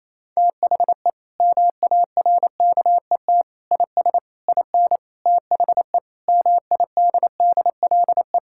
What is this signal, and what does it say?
Morse code, heard as a keyed tone, about 28 wpm, 700 Hz. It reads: THE MARKET IS IN THE MIDDLE
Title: Sentences from Top 1000 Words - Review Part 6 - 28wpm